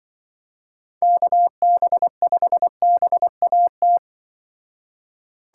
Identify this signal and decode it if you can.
Morse code, heard as a keyed tone, about 24 wpm, 700 Hz. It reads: KB5BAT
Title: Speed Racing - USA Call Signs 20wpm